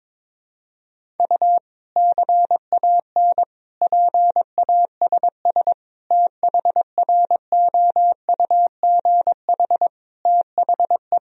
Morse code, heard as a keyed tone, about 21 wpm, 700 Hz. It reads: U CAN PASS THROUGH THE
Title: Sentences from Top 400 Words and Abbreviations 22wpm